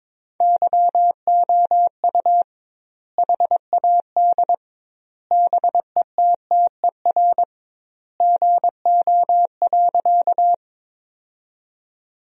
Morse code, heard as a keyed tone, about 22 wpm, 700 Hz. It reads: YOU HAD BETTER GO.
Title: 2x Word-Spacing - Sentences from Top 400 Words - 22wpm